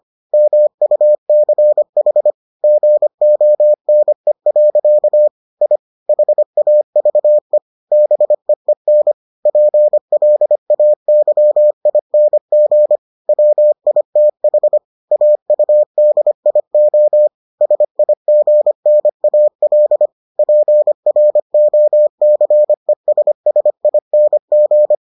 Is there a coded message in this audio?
Yes: Morse code from 0.3 to 24.9 s, heard as a keyed tone, about 25 wpm, 600 Hz. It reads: MUCH GONE. I HAVE BEEN PLAYING WITH AUDIO SIGNAL PROCESSING